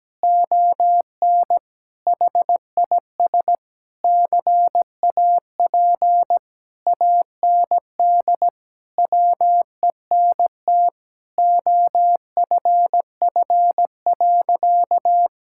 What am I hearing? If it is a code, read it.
Morse code, heard as a keyed tone, about 17 wpm, 700 Hz. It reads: ON HIS CAP AND WENT OFF.
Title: Sentences from Top 1600 Words 17wpm